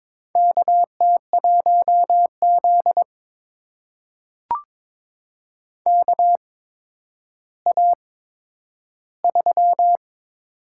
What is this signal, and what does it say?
Morse code, heard as a keyed tone, about 22 wpm, spaced as for 11 wpm, 700 Hz. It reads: KT1Z KA3
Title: USA Call Signs - 4 Characters - 8x Character Spacing - 22wpm